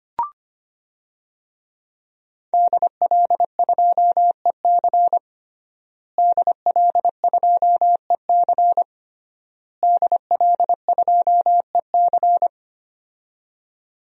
Morse code, heard as a keyed tone, about 25 wpm, 700 Hz. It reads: DL2EC DL2EC DL2EC
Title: DXCC-100 Call Signs - 5 Characters - Repeated 3x - 25wpm